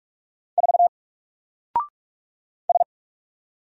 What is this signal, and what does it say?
Morse code, heard as a keyed tone, about 44 wpm, 700 Hz. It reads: 4 S
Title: Single Letter-Number Alt 45wpm